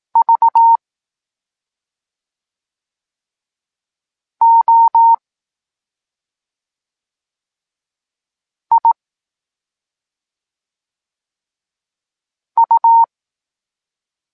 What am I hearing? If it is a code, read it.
Morse code, heard as a keyed tone, about 18 wpm, 920 Hz. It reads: V O I U
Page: Tekst 8 – doskonalenie odbioru poznanych liter